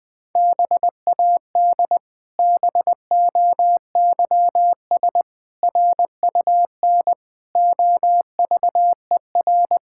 Morse code, heard as a keyed tone, about 21 wpm, 700 Hz. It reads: BAD BOYS RUN OVER
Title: Pangram Challenge v2 - 20wpm